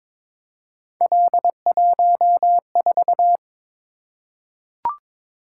Morse code, heard as a keyed tone, about 22 wpm, 700 Hz. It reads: L14
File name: practice / 22wpm / Three Letter-Number Combinations - 22wpm